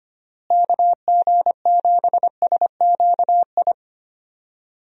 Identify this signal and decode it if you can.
Morse code, heard as a keyed tone, about 25 wpm, 700 Hz. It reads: KG7SQI
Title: USA Call Signs - 6 Characters - Repeated 3x - 25wpm